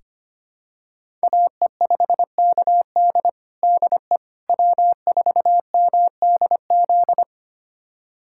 Morse code, heard as a keyed tone, about 25 wpm, 700 Hz. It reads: AE5KD DE W4MDZ
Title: USA Callsign DE Callsign - 5 Characters - 25wpm